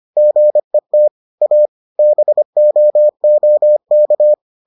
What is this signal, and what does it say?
Morse code, heard as a keyed tone, about 25 wpm, 600 Hz. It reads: GET A BOOK